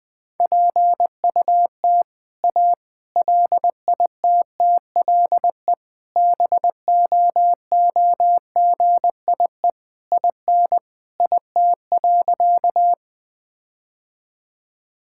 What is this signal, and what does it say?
Morse code, heard as a keyed tone, about 20 wpm, 700 Hz. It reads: PUT A LITTLE BOOGIE IN IT.
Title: All-Age Appropriate Jokes - 20wpm